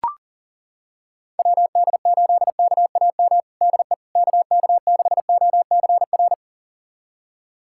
Morse code, heard as a keyed tone, about 40 wpm, 700 Hz. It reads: WD8KAM DE KK6OCR
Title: USA Callsign DE Callsign 40wpm